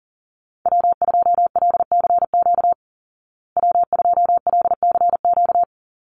Morse code, heard as a keyed tone, about 40 wpm, 700 Hz. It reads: W2LCQ W2LCQ
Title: Call Signs Repeated 2x - 40wpm